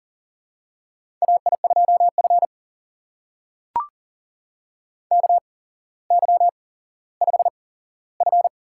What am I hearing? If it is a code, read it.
Morse code, heard as a keyed tone, about 40 wpm, spaced as for 17 wpm, 700 Hz. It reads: AI2F KY5F
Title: USA Call Signs - 4 Characters - 8x Character Spacing - 40wpm